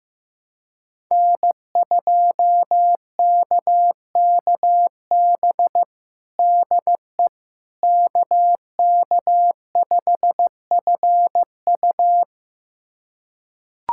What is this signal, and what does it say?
Morse code, heard as a keyed tone, about 15 wpm, 700 Hz. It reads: N2KKB DE KK5FU
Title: USA Callsign DE Callsign 15wpm